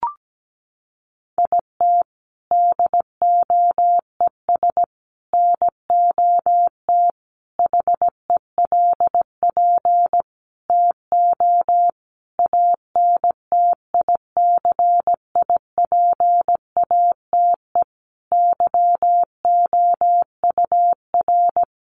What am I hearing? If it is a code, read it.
Morse code, heard as a keyed tone, about 17 wpm, 700 Hz. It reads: IT DOES NOT HELP TO ANTICIPATE YOUR
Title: Sentences from Top 1500 Words 17wpm